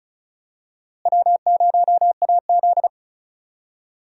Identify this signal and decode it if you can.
Morse code, heard as a keyed tone, about 34 wpm, 700 Hz. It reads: W0AZ
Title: USA Call Signs - 4 Characters - Repeated 3x - 35wpm